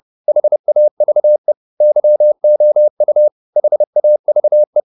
Morse code, heard as a keyed tone, about 30 wpm, 600 Hz. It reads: HAVE YOU HAVE